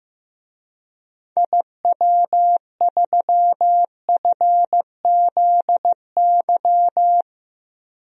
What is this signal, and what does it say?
Morse code, heard as a keyed tone, about 15 wpm, 700 Hz. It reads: IW3FZY
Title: DXCC-100 Call Signs - 6 Characters - Repeated 3x - 15wpm